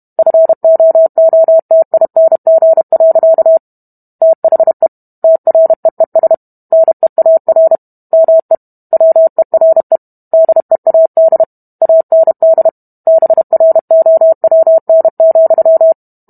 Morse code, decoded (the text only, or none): FOOTING. THE TREES NEAR ME WERE DEAD AND BROWN,